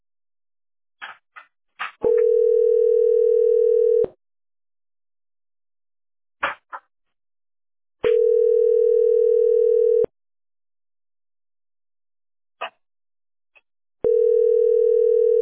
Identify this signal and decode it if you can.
Ringback tone (North America), 440 + 480 Hz.